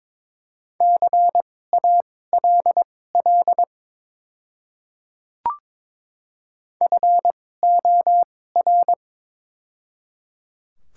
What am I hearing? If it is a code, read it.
Morse code, heard as a keyed tone, about 23 wpm, spaced as for 16 wpm, 700 Hz. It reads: CALL FOR
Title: 2x Character Spacing - Top 300 Words - 22wpm